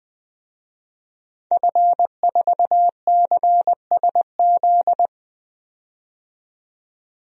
Morse code, heard as a keyed tone, about 20 wpm, 700 Hz. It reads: F4CSZ
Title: DXCC-100 Call Signs - 5 Characters - 20wpm